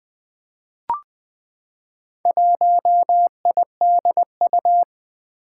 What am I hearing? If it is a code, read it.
Morse code, heard as a keyed tone, about 20 wpm, 700 Hz. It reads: 1IDU
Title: Four Challenging Letter-Number Combinations 20wpm